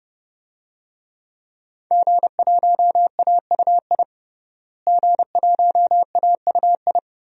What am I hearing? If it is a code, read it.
Morse code, heard as a keyed tone, about 32 wpm, 700 Hz. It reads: G1AUI G1AUI
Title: DXCC-100 Call Signs - 5 Characters - Repeated 2x - 30wpm